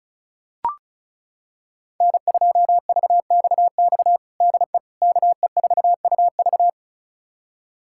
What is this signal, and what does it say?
Morse code, heard as a keyed tone, about 36 wpm, 700 Hz. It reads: N2VXX DE KE4UV
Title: USA Callsign DE Callsign - 5 Characters - 35wpm